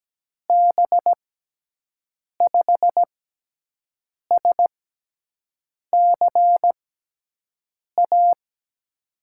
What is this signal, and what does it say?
Morse code, heard as a keyed tone, about 17 wpm, spaced as for 6 wpm, 700 Hz. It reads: B5SCA
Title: USA Call Signs - 6 Characters - 6x Character Spacing - 17wpm